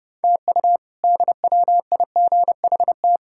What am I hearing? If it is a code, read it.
Morse code, heard as a keyed tone, about 30 wpm, 700 Hz. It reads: TU DWIGHT
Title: K1USN SST QSO Practice - 30wpm